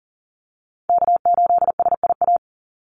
Morse code, heard as a keyed tone, about 40 wpm, 700 Hz. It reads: K8SIA
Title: Call Signs Repeated 2x - 40wpm